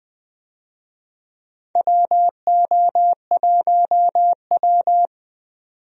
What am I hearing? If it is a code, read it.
Morse code, heard as a keyed tone, about 21 wpm, 700 Hz. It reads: WO1W
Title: USA Call Signs - 4 Characters - 20wpm